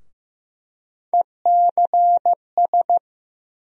Morse code, heard as a keyed tone, about 15 wpm, 700 Hz. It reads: ECS
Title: USA Call Sign Postfixes - 3 Letters 15wpm